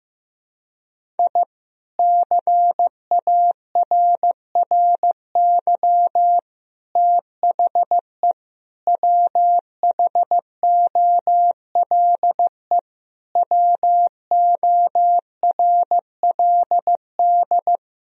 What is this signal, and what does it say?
Morse code, heard as a keyed tone, about 15 wpm, 700 Hz. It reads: I CARRY THE WHOLE WORLD
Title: Sentences from Top 1000 Words - Review Part 1 - 15wpm